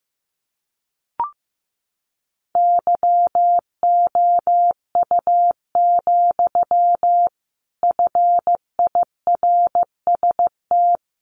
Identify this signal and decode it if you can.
Morse code, heard as a keyed tone, about 15 wpm, 700 Hz. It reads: YOU, FIRST
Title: A Sound Of Thunder 15wpm